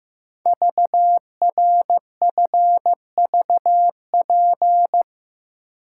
Morse code, heard as a keyed tone, about 15 wpm, 700 Hz. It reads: VRFVP
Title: Code Groups - 5 Letters - 15wpm